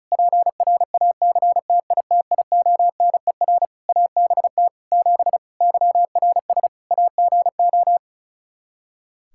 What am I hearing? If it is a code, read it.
Morse code, heard as a keyed tone, about 34 wpm, 700 Hz. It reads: PRACTITIONER ABT 7 YRS AGO
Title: Practice QSO - Exchange 3 - More Info - 35wpm